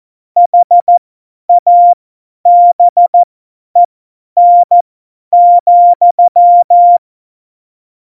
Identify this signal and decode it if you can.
Morse code, heard as a keyed tone, about 14 wpm, spaced as for 10 wpm, 700 Hz. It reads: HABEN,
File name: Morsekurs / Witze / Witz 22. September 2020, 50 F70